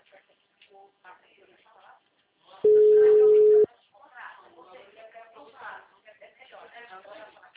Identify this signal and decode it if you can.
Ringback tone (Europe, 425 Hz), with speech.
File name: out- -2044- - - .61747